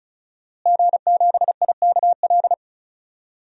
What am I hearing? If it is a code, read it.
Morse code, heard as a keyed tone, about 36 wpm, 700 Hz. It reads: G7IKL
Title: DXCC-100 Call Signs - 5 Characters - Repeated 2x - 35wpm